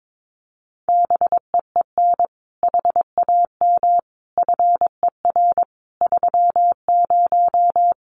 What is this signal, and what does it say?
Morse code, heard as a keyed tone, about 22 wpm, 700 Hz. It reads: BEEN HAM FER 30
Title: Common QSO Phrases Encore 22wpm